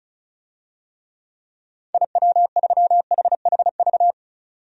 Morse code, heard as a keyed tone, about 36 wpm, 700 Hz. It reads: IW3HHV
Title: DXCC-100 Call Signs - 6 Characters - 35wpm